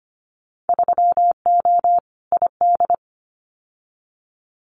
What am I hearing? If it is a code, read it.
Morse code, heard as a keyed tone, about 25 wpm, 700 Hz. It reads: 3O ID
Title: Winter Field Day Exchanges 25wpm